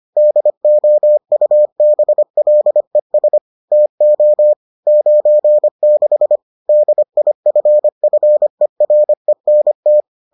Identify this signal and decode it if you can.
Morse code, heard as a keyed tone, about 25 wpm, 600 Hz. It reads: DOUBLES TO 96 DIFFERENT